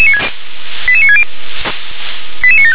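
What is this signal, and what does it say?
Five-tone selective call (ZVEI), heard as quick runs of five tones, tones of about 70 ms. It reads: 88960 88975
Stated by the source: ZVEI five-tone call received on SDR.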